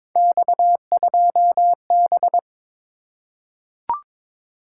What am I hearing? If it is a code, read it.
Morse code, heard as a keyed tone, about 22 wpm, 700 Hz. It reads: X2B
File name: practice / 22wpm / Three Letter-Number Combinations - 22wpm